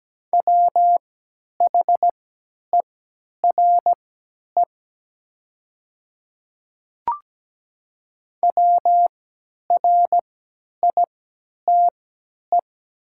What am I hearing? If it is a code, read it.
Morse code, heard as a keyed tone, about 17 wpm, spaced as for 10 wpm, 700 Hz. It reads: WHERE WRITE
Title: 3x Character Spacing - Top 300 Words - 17wpm